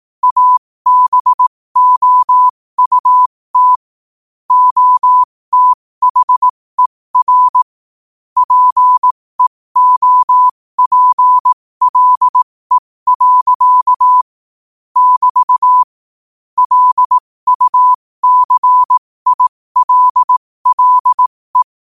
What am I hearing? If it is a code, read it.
Morse code, heard as a keyed tone, about 18 wpm, spaced as for 15 wpm, 1000 Hz. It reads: ABOUT OTHER PEOPLE. <BT> LUCILLE
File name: qotdfiles / 15wpm